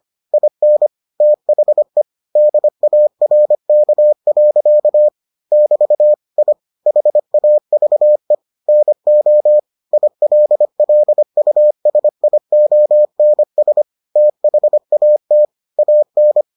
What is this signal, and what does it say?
Morse code, heard as a keyed tone, about 25 wpm, 600 Hz. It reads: IN THE DARK. <BT> I HAVE NO ILLUSIONS THAT AN